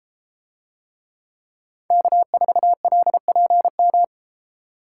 Morse code, heard as a keyed tone, about 33 wpm, 700 Hz. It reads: K4LPM